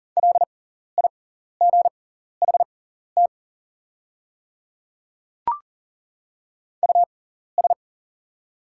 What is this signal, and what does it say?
Morse code, heard as a keyed tone, about 38 wpm, spaced as for 14 wpm, 690 Hz. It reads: LIGHT US
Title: 6x Character Spacing - Top 300 Words - 40wpm